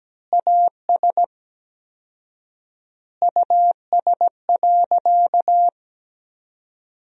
Morse code, heard as a keyed tone, about 17 wpm, 700 Hz. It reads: AS US.
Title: 4x Word-Spacing - Sentences from Top 500 Words - 17wpm